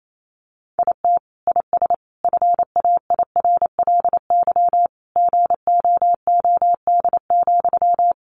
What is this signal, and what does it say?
Morse code, heard as a keyed tone, about 28 wpm, 700 Hz. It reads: IT IS FAIRLY GOOD,
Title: Sentences from Top 1100 Words 28wpm